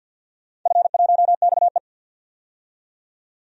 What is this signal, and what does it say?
Morse code, heard as a keyed tone, about 51 wpm, 700 Hz. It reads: F1XE
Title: DXCC-100 Call Signs - 4 Characters - 50wpm